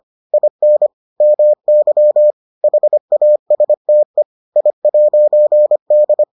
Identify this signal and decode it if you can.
Morse code, heard as a keyed tone, about 25 wpm, 600 Hz. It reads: IN MY HASTE I'D